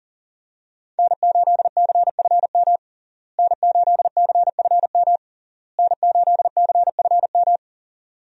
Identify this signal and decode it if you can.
Morse code, heard as a keyed tone, about 37 wpm, 700 Hz. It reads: N8CFM N8CFM N8CFM
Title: USA Call Signs - 5 Characters - Repeated 3x - 40wpm